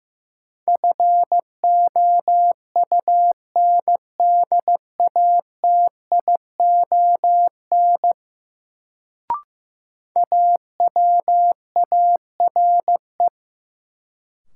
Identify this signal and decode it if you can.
Morse code, heard as a keyed tone, about 15 wpm, 700 Hz. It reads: FOUNDATION AWARE
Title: Sentences from Top 1200 Words - New Word Drill Alt - 15wpm